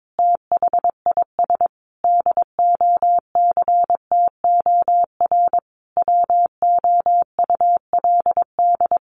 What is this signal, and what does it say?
Morse code, heard as a keyed tone, about 22 wpm, 700 Hz. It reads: THIS DOCTOR WOULD
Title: Sentences from Top 1500 Words 22wpm